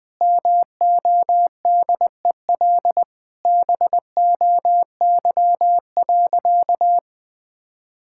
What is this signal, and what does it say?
Morse code, heard as a keyed tone, about 21 wpm, 700 Hz. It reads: MODEL BOY.
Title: Sentences from Top 1000 Words - Review Part 5 - 20wpm